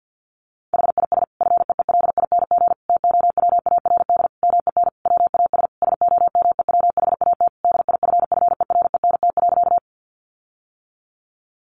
Morse code, heard as a keyed tone, about 48 wpm, 700 Hz. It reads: HIS FEELING TOWARD MEN WAS SOMEWHAT DIFFERENT.
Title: Sentences from Top 1300 Words 50wpm